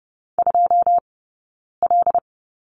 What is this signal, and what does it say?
Morse code, heard as a keyed tone, about 30 wpm, spaced as for 9 wpm, 690 Hz. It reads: 2L
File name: Lesson 18d - Letter m - Two Characters 30wpm